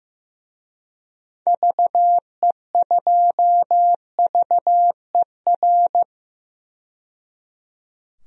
Morse code, heard as a keyed tone, about 15 wpm, 700 Hz. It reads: VE2VER
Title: DXCC-100 Call Signs - 6 Characters - 15wpm